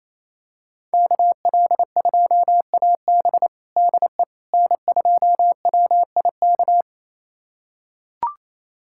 Morse code, heard as a keyed tone, about 28 wpm, 700 Hz. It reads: KL2AB DE N2WIK
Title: USA Callsign DE Callsign - 5 Characters - 28wpm